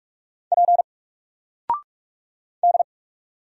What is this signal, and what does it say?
Morse code, heard as a keyed tone, about 44 wpm, 700 Hz. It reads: P D
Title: Single Letter-Number 45wpm